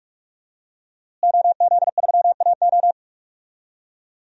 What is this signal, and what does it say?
Morse code, heard as a keyed tone, about 43 wpm, 700 Hz. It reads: OZ3AO
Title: DXCC-100 Call Signs - 5 Characters - Repeated 2x - 45wpm